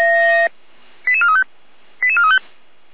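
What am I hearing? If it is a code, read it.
Five-tone selective call (ZVEI), heard as quick runs of five tones, tones of about 70 ms. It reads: 88435 88435
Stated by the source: ZVEI five-tone selective call 88291, received by SDR.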